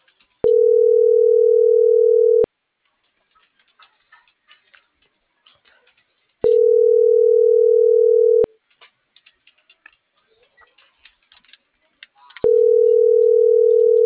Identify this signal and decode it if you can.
Ringback tone (North America), 440 + 480 Hz.